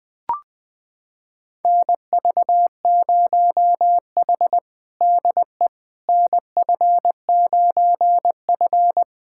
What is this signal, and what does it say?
Morse code, heard as a keyed tone, about 21 wpm, 700 Hz. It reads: NV0H DE NF9F
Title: USA Callsign DE Callsign - 4 Characters - 20wpm